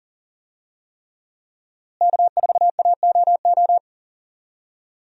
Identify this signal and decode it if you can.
Morse code, heard as a keyed tone, about 40 wpm, 700 Hz. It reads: K4AOO
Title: USA Call Signs - 5 Characters - Repeated 3x - 40wpm